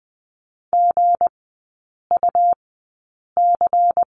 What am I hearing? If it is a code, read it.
Morse code, heard as a keyed tone, about 20 wpm, spaced as for 8 wpm, 700 Hz. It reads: GUC
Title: 159 Lesson 26e - Letter g - Three Characters 20wpm